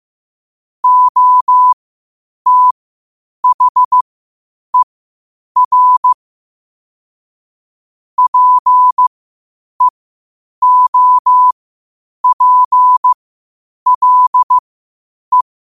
Morse code, heard as a keyed tone, about 15 wpm, spaced as for 8 wpm, 1000 Hz. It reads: OTHER PEOPLE
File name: qotdfiles / 8wpm